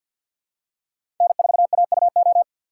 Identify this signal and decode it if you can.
Morse code, heard as a keyed tone, about 52 wpm, 690 Hz. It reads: N4AUO
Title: USA Call Signs - 5 Characters - 4x Character Spacing - 50wpm